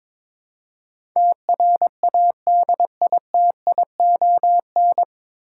Morse code, heard as a keyed tone, about 22 wpm, 700 Hz. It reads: TRADITION
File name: 9 Letter Words - First 3 Letters Given - 22wpm